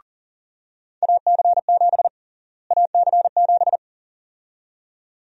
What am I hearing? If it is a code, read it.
Morse code, heard as a keyed tone, about 40 wpm, 700 Hz. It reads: AC7 AC7
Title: USA Call Sign Prefixes - Repeated 2x - 40wpm